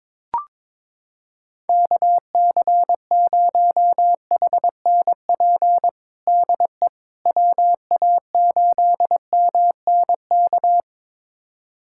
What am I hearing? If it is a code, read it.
Morse code, heard as a keyed tone, about 22 wpm, 700 Hz. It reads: KC0HNP DE WA8MNK
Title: USA Callsign DE Callsign 22wpm